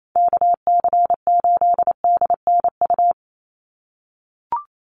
Morse code, heard as a keyed tone, about 28 wpm, 700 Hz. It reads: KC8DNU
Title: USA Call Signs 28wpm